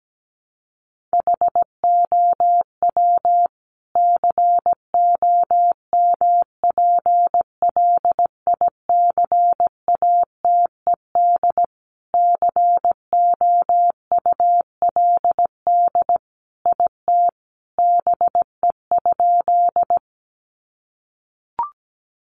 Morse code, heard as a keyed tone, about 17 wpm, 700 Hz. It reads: HOW COMPLICATED COULD IT BE?
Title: Sentences from Top 1200 Words 17wpm